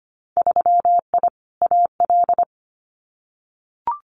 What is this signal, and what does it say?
Morse code, heard as a keyed tone, about 25 wpm, 700 Hz. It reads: 3I AL
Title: Winter Field Day Exchanges Encore 25wpm